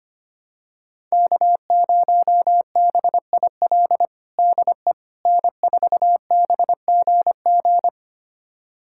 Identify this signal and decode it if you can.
Morse code, heard as a keyed tone, about 25 wpm, 700 Hz. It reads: K0BIL DE N4BGG
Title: USA Callsign DE Callsign - 5 Characters - 25wpm